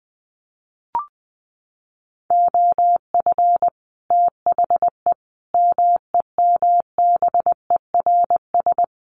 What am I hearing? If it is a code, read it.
Morse code, heard as a keyed tone, about 20 wpm, 700 Hz. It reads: OF THE MEMBERS
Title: Sets of 5 Words Encore2 20wpm